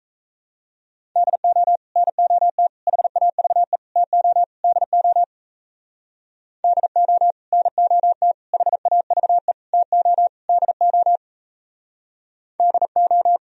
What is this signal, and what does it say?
Morse code, heard as a keyed tone, about 37 wpm, 700 Hz. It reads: DO NOT HAVE TO DO DO NOT HAVE TO DO DO